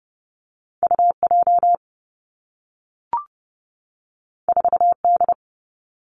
Morse code, heard as a keyed tone, about 30 wpm, 700 Hz. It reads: U J 4 D
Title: Two Challenging Letter-Number Combinations 30wpm